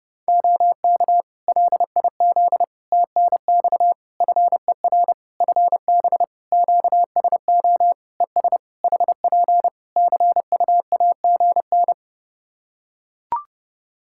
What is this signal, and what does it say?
Morse code, heard as a keyed tone, about 30 wpm, 700 Hz. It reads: OK LIZ TNX FER FB QSO ES HP CUAGN
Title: Practice QSO - Exchange 4 - Ending - 30wpm